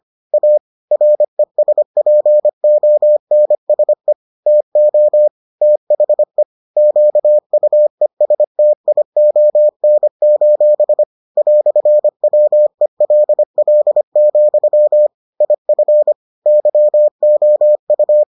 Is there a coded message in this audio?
Yes: Morse code from 0.3 to 18.2 s, heard as a keyed tone, about 25 wpm, 600 Hz. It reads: A RESPONSE TO THE QUESTION: "WELL, IF YOU